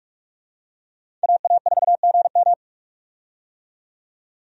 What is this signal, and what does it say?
Morse code, heard as a keyed tone, about 46 wpm, 700 Hz. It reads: AA4GM
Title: USA Call Signs - 5 Characters - 45wpm